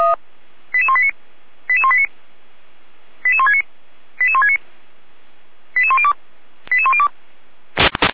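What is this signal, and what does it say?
Five-tone selective call (ZVEI), heard as quick runs of five tones, tones of about 70 ms. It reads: 88189 88189 88169 88169 88102 88102
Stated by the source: ZVEI five-tone selective call received on SDR.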